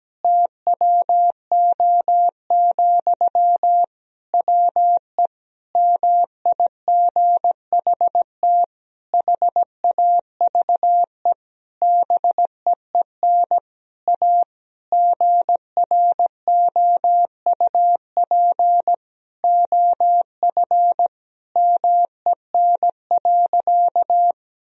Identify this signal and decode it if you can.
Morse code, heard as a keyed tone, about 17 wpm, 700 Hz. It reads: TWO, WE MIGHT HAVE BEEN A GROUP OF MEN.
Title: Sentences from Top 1000 Words - Review Part 1 - 17wpm